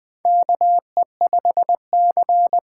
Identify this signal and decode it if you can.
Morse code, heard as a keyed tone, about 20 wpm, 700 Hz. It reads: KE5C
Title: USA Call Signs - 6x Character Spacing - 20wpm